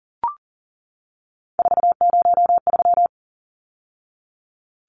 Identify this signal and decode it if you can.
Morse code, heard as a keyed tone, about 40 wpm, 700 Hz. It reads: 403
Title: Numbers - 3-Digit 40wpm